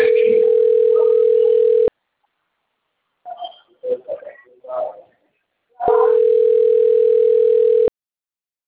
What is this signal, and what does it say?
Ringback tone (North America), 440 + 480 Hz.